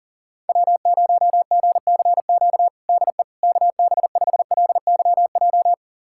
Morse code, heard as a keyed tone, about 40 wpm, 700 Hz. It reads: W0GCQ DE KB5LYJ
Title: USA Callsign DE Callsign 40wpm